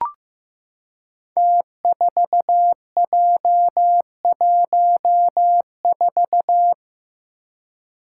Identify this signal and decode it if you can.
Morse code, heard as a keyed tone, about 15 wpm, 700 Hz. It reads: T4J14
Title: Five Challenging Letter-Number Combinations 15wpm